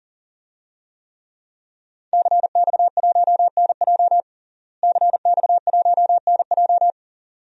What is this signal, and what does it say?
Morse code, heard as a keyed tone, about 40 wpm, 700 Hz. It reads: CX1NJ CX1NJ
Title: DXCC-100 Call Signs - 5 Characters - Repeated 2x - 40wpm